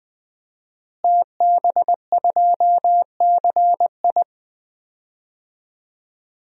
Morse code, heard as a keyed tone, about 21 wpm, 700 Hz. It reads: TB2CI
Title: DXCC-100 Call Signs - 5 Characters - Repeated 2x - 20wpm